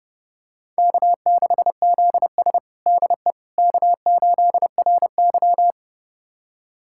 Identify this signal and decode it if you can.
Morse code, heard as a keyed tone, about 30 wpm, 700 Hz. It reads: K6ZS DE K8RY
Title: USA Callsign DE Callsign - 4 Characters - 30wpm